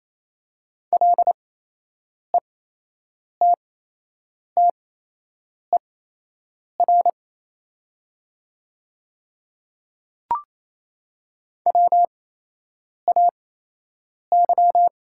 Morse code, heard as a keyed tone, about 28 wpm, spaced as for 8 wpm, 700 Hz. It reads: LETTER WAY
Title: 8x Character Spacing - Top 300 Words - 28wpm